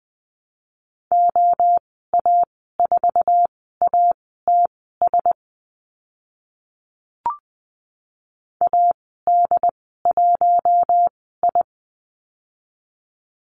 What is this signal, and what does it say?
Morse code, heard as a keyed tone, about 20 wpm, spaced as for 14 wpm, 700 Hz. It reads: OA4ATS AD1I
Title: 069 Lesson 12g - Letter d - Call Signs 20wpm